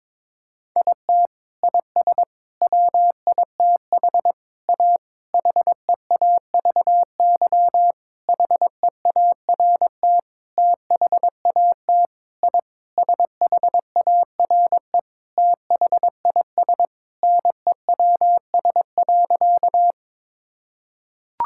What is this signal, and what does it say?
Morse code, heard as a keyed tone, about 22 wpm, 700 Hz. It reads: IT IS WITH A HEAVY HEART THAT I SHARE THIS NEWS.
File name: practice / 22wpm / Sentences with 8-Word Phrases - 22wpm